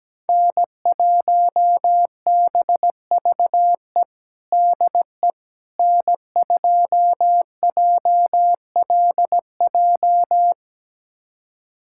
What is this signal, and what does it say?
Morse code, heard as a keyed tone, about 17 wpm, 700 Hz. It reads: N1BVE DE N2JLJ
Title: USA Callsign DE Callsign - 5 Characters - 17wpm